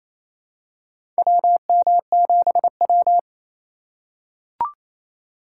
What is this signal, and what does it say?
Morse code, heard as a keyed tone, about 28 wpm, 700 Hz. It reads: WM7W
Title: USA Call Signs - 6x Character Spacing - 28wpm